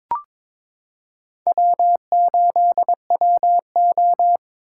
Morse code, heard as a keyed tone, about 22 wpm, 700 Hz. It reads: W8WO
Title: USA Callsign DE Callsign - 4 Characters - 22wpm